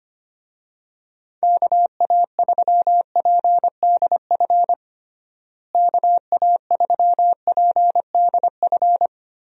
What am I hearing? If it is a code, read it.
Morse code, heard as a keyed tone, about 25 wpm, 700 Hz. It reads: KA3PDF KA3PDF
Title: USA Call Signs - 6 Characters - Repeated 3x - 25wpm